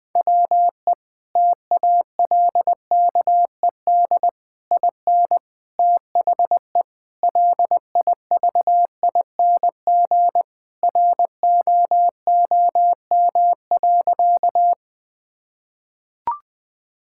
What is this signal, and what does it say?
Morse code, heard as a keyed tone, about 20 wpm, 700 Hz. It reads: WE TALKED IN THE LIVING ROOM.
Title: Sentences with 4-Word Phrases - 20wpm